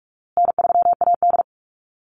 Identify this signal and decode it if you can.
Morse code, heard as a keyed tone, about 44 wpm, 700 Hz. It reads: N3AD
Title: Call Signs Repeated 2x - 45wpm